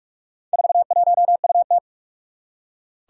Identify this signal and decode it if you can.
Morse code, heard as a keyed tone, about 41 wpm, 700 Hz. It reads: 41UT
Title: Four Challenging Letter-Number Combinations 45wpm